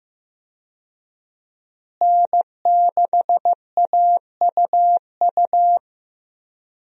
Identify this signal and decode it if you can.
Morse code, heard as a keyed tone, about 15 wpm, 700 Hz. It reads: N6AUU
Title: USA Call Signs - 5 Characters - Repeated 3x - 15wpm